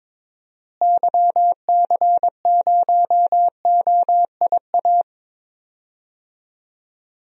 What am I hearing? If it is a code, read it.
Morse code, heard as a keyed tone, about 22 wpm, 700 Hz. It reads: YC0OIA
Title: DXCC-100 Call Signs - 6 Characters - Repeated 2x - 22wpm